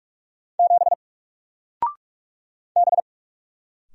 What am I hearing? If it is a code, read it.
Morse code, heard as a keyed tone, about 46 wpm, 700 Hz. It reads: 7 B
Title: Single Letter-Number 45wpm